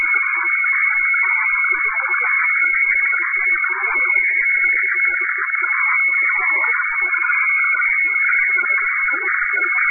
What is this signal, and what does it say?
Weather fax (radiofax).
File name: hfwefax / 12788.1.khz. .2025